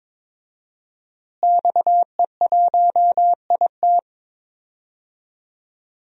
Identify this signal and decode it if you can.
Morse code, heard as a keyed tone, about 22 wpm, 700 Hz. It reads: XE1IT
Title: DXCC-100 Call Signs - 5 Characters - Repeated 2x - 22wpm